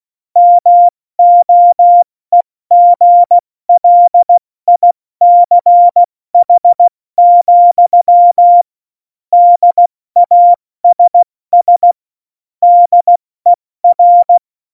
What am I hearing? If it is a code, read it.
Morse code, heard as a keyed tone, about 16 wpm, 700 Hz. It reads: MOEGLICH, DASS DER
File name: Morsekurs / Witze / Witz 14 Juli 2020, 70 F80